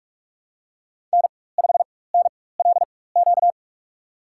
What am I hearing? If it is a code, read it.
Morse code, heard as a keyed tone, about 43 wpm, spaced as for 21 wpm, 700 Hz. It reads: N5NLQ
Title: USA Call Signs - 5 Characters - 4x Character Spacing - 45wpm